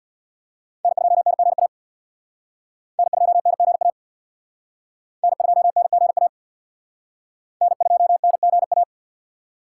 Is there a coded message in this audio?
Yes: Morse code from 0.8 to 8.8 s, heard as a keyed tone, about 63 wpm, 700 Hz. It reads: N2NGA N2NGA N2NGA N2NGA